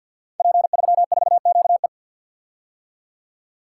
Morse code, heard as a keyed tone, about 52 wpm, 700 Hz. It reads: P34QE
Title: Code Groups - 5 Letter:Numbers - 50wpm